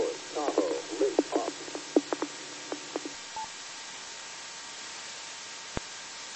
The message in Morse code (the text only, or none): none